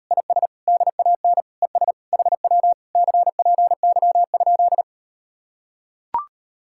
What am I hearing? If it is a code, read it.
Morse code, heard as a keyed tone, about 38 wpm, 700 Hz. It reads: IS DAN ES HW CPY?